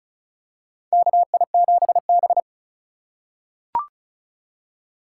Morse code, heard as a keyed tone, about 36 wpm, 700 Hz. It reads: KI7B
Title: USA Call Signs - 4 Characters - 4x Character Spacing - 35wpm